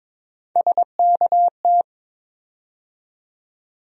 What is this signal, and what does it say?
Morse code, heard as a keyed tone, about 22 wpm, 700 Hz. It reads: SKT
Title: USA Call Sign Postfixes - 3 Letters 22wpm